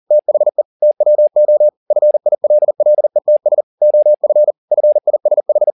Morse code, heard as a keyed tone, about 40 wpm, 600 Hz. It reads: THE TWO FILLETS OF FISH
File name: the old man and the sea 2 40wpm